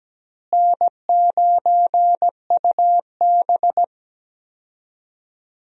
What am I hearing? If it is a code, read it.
Morse code, heard as a keyed tone, about 17 wpm, 700 Hz. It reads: N9UB
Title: USA Call Signs - 4 Characters - Repeated 3x - 17wpm